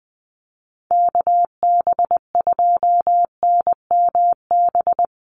Morse code, heard as a keyed tone, about 20 wpm, 700 Hz. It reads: KB2NMB
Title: USA Call Signs Repeated 2x 20wpm